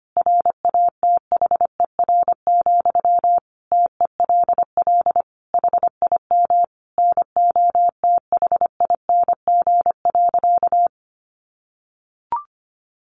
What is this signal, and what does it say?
Morse code, heard as a keyed tone, about 25 wpm, 700 Hz. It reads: RATHER, TELL HIM NOTHING.
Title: Sentences from Top 700 Words 25wpm